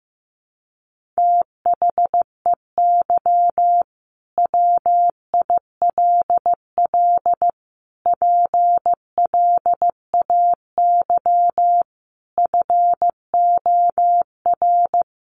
Morse code, heard as a keyed tone, about 15 wpm, 700 Hz. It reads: THEY WILL PLAY FOR
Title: Sentences from Top 1500 Words 15wpm